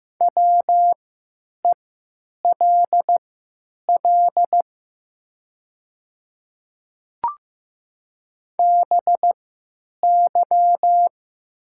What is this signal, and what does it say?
Morse code, heard as a keyed tone, about 15 wpm, spaced as for 9 wpm, 700 Hz. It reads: WELL BY
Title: 3x Character Spacing - Top 200 Words - 15wpm